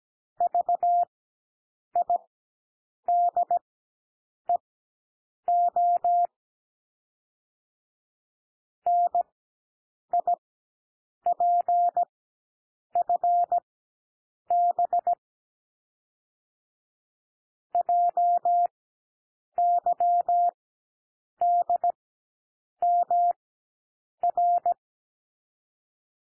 Morse code, heard as a keyed tone, about 17 wpm, spaced as for 7 wpm, 700 Hz. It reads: VIDEO NIPFB JYDMR
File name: SOUND LETTERS 17 wpm